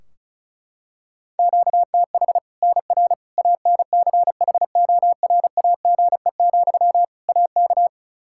Morse code, heard as a keyed tone, about 34 wpm, 700 Hz. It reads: QTH NR ANCHORAGE, AK
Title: K1USN SST QSO Practice 35wpm